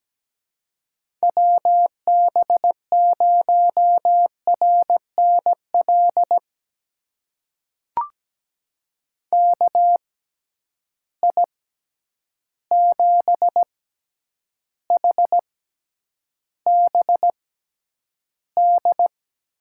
Morse code, heard as a keyed tone, about 17 wpm, spaced as for 9 wpm, 700 Hz. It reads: WB0RNL KI7HBD
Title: USA Call Signs - 6 Characters - 6x Character Spacing - 17wpm